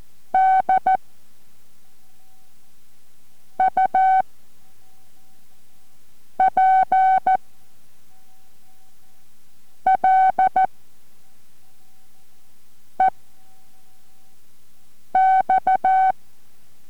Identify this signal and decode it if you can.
Morse code, heard as a keyed tone, about 14 wpm, 760 Hz. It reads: D U P L E X